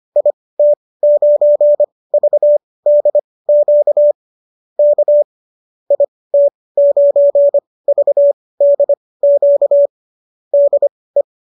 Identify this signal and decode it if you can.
Morse code, heard as a keyed tone, about 25 wpm, spaced as for 18 wpm, 590 Hz. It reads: IT9VDQ K IT9VDQ DE